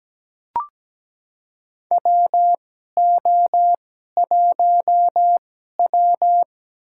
Morse code, heard as a keyed tone, about 17 wpm, spaced as for 12 wpm, 700 Hz. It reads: WO1W
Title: USA Call Signs - 4 Characters - 2x Character Spacing - 17wpm